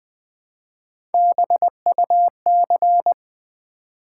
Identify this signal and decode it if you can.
Morse code, heard as a keyed tone, about 20 wpm, 700 Hz. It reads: BUC
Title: Build A Word - 1 Letter At A Time - 4-Letter Words - 20wpm